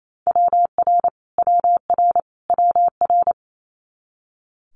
Morse code, heard as a keyed tone, about 28 wpm, 700 Hz. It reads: WR WR WR
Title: Multi-letter Phonemes Repeated 3x 28wpm